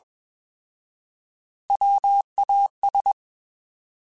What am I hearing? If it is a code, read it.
Morse code, heard as a keyed tone, about 21 wpm, 800 Hz. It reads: WAS
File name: Alphabetical / Alphabetical mp3s / Ws @20